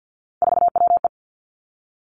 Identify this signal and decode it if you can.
Morse code, heard as a keyed tone, about 50 wpm, 700 Hz. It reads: 4WE
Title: Three Challenging Letter-Number Combinations 50wpm